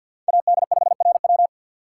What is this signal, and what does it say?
Morse code, heard as a keyed tone, about 51 wpm, 700 Hz. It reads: AD5RW
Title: USA Call Signs - 5 Characters - 2x Character Spacing - 50wpm